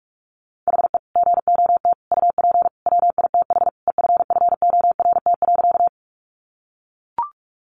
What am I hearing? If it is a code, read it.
Morse code, heard as a keyed tone, about 44 wpm, 700 Hz. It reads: HE GOT UP WITH EFFORT.
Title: Sentences from Top 1200 Words 45wpm